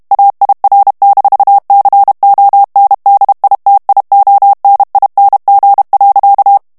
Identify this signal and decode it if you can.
Morse code, heard as a keyed tone, about 32 wpm, 800 Hz. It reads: AIR-CONDITIONING.
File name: ANGLAIS texte clair 2 vitesse 30 mots minute